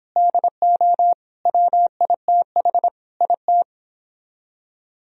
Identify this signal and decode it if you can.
Morse code, heard as a keyed tone, about 29 wpm, 700 Hz. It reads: DO WITH IT